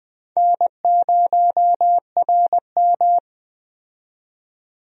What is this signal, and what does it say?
Morse code, heard as a keyed tone, about 21 wpm, 700 Hz. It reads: N0RM
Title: USA Call Signs - 4 Characters - 20wpm